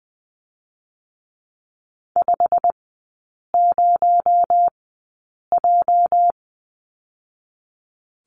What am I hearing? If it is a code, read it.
Morse code, heard as a keyed tone, about 20 wpm, spaced as for 8 wpm, 700 Hz. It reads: 50J
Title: 218 Lesson 35e - Number 0 - Three Characters 20wpm